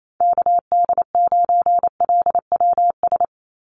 Morse code, heard as a keyed tone, about 28 wpm, 700 Hz. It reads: KD9LWS
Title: USA Call Signs 28wpm